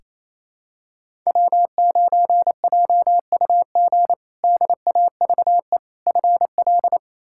Morse code, heard as a keyed tone, about 28 wpm, 700 Hz. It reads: W9JUG DAVE FL
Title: K1USN SST QSO Practice - 28wpm